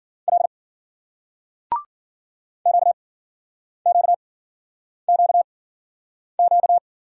Morse code, heard as a keyed tone, about 56 wpm, spaced as for 9 wpm, 700 Hz. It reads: L QQQQ